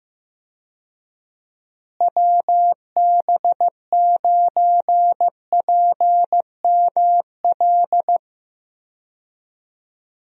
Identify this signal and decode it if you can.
Morse code, heard as a keyed tone, about 15 wpm, 700 Hz. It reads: WB9PML
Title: USA Call Signs - 6 Characters - Repeated 2x - 15wpm